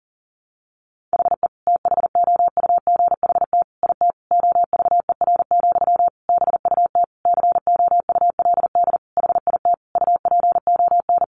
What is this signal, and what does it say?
Morse code, heard as a keyed tone, about 40 wpm, 700 Hz. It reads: HE THOUGHT IT OVER, BUT COULD HIT UPON
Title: Sentences from Top 1200 Words 40wpm